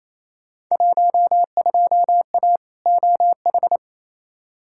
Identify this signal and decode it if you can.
Morse code, heard as a keyed tone, about 28 wpm, 700 Hz. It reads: 12A OH
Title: Field Day Exchanges Encore - 28wpm